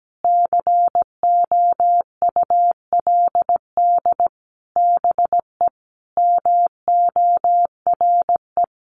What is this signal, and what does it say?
Morse code, heard as a keyed tone, about 17 wpm, 700 Hz. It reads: COULD BE MORE
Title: Sentences from Top 900 Words 17wpm